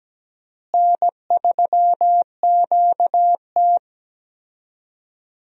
Morse code, heard as a keyed tone, about 17 wpm, 700 Hz. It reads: N3QT
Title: USA Call Signs - 4 Characters - Repeated 3x - 17wpm